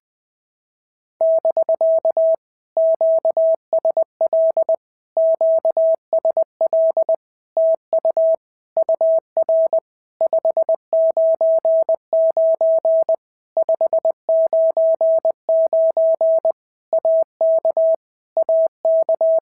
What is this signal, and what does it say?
Morse code, heard as a keyed tone, about 20 wpm, 650 Hz. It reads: <BK> QSL QSL TU UR 599 599 AK AK
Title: POTA QSO Practice - 20wpm